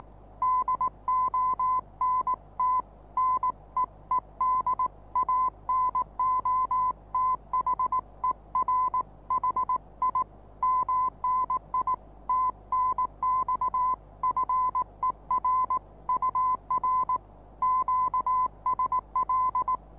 Morse code, heard as a keyed tone, about 19 wpm, 1030 Hz. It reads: DONT NEED ANOTHER HI MNI TNX FER UR QSL
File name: repos / Morsekurs / DJ4UF Morsen26